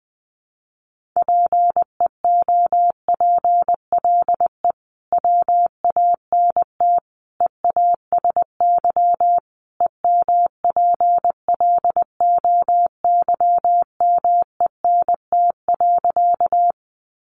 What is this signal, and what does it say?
Morse code, heard as a keyed tone, about 19 wpm, 700 Hz. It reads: PEOPLE WANT EASY EMPLOYMENT.
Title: Sentences from Top 1500 Words 20wpm